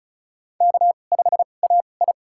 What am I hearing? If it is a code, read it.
Morse code, heard as a keyed tone, about 36 wpm, spaced as for 25 wpm, 700 Hz. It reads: K5AI
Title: USA Call Signs - 4 Characters - 2x Character Spacing - 35wpm